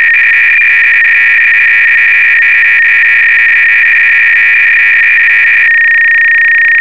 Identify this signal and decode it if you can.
300-baud modem data.
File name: Repository / ACR-Audio-Cassette-Programs / MS-16k-BASICV4